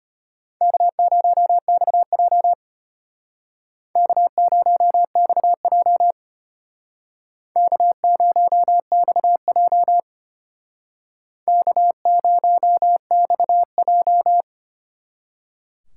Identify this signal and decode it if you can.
Morse code, heard as a keyed tone, about 31 wpm, 700 Hz. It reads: K0XJ K0XJ K0XJ K0XJ
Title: Speed Racing - USA 1x2 Call Signs 25wpm